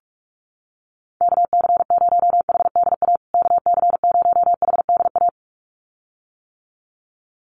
Morse code, heard as a keyed tone, about 44 wpm, 700 Hz. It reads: KC0HDA KC0HDA
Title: USA Call Signs Repeated 2x 45wpm